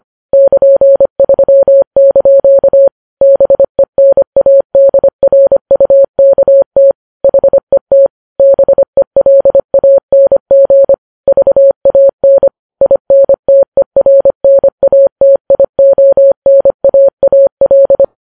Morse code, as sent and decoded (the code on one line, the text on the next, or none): -.--. ...-- -.--.- / -... -. .- -.. .-. ..- -.- - / .... - / -... .-.. .- -. --. / ...- .- -. / .. -. - . .-. -. .- - .. --- -. .- .- .-..
<KN>3) BENADRUKT HET BELANG VAN INTERNATIONAAL